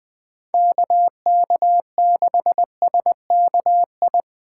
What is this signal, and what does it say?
Morse code, heard as a keyed tone, about 20 wpm, 700 Hz. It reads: KK6SKI